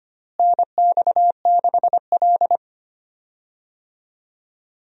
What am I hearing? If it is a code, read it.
Morse code, heard as a keyed tone, about 25 wpm, 700 Hz. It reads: NX6L
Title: USA Call Signs - 4 Characters - Repeated 2x - 25wpm